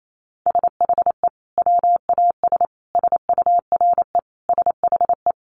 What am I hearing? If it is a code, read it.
Morse code, heard as a keyed tone, about 28 wpm, 700 Hz. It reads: SHE WAS SURE SHE
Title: Sentences from Top 1500 Words 28wpm